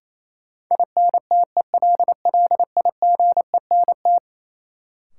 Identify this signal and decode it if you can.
Morse code, heard as a keyed tone, about 28 wpm, 700 Hz. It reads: INTELLIGENT
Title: Sentences from Top 1000 Words - Review Part 7 - New Word Drill - 28wpm